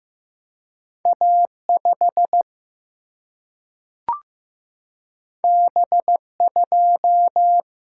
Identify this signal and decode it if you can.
Morse code, heard as a keyed tone, about 15 wpm, 700 Hz. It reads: A5 B2
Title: Single-Letter Single-Number 15wpm